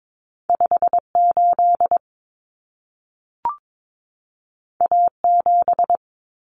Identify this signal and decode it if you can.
Morse code, heard as a keyed tone, about 22 wpm, 700 Hz. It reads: 58 A7
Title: Common Cut Numbers - 2-Digit 22wpm